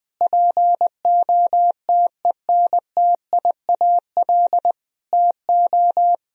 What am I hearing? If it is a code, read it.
Morse code, heard as a keyed tone, about 20 wpm, 700 Hz. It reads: POTENTIAL TO
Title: Sentences from Top 1000 Words - Review Part 4 - 20wpm